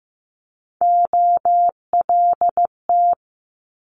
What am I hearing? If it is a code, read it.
Morse code, heard as a keyed tone, about 15 wpm, 700 Hz. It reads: OLT
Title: Multi-letter Phonemes Quick Fire 15wpm